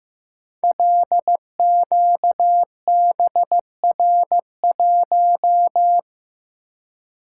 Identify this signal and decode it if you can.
Morse code, heard as a keyed tone, about 15 wpm, 700 Hz. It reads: LQBR1
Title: Code Groups - 5 Letter:Numbers - 15wpm